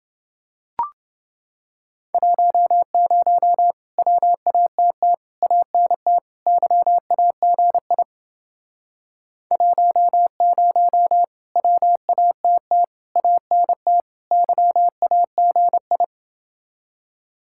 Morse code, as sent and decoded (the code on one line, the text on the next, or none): .---- ----- / .-- .- - - / .- -. - / -.-- .- --. .. / .---- ----- / .-- .- - - / .- -. - / -.-- .- --. ..
10 WATT ANT YAGI 10 WATT ANT YAGI